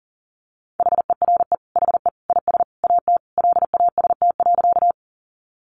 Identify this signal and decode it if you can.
Morse code, heard as a keyed tone, about 40 wpm, 700 Hz. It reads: HERE HE IS AT LAST.
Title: Sentences from Top 300 Words 40wpm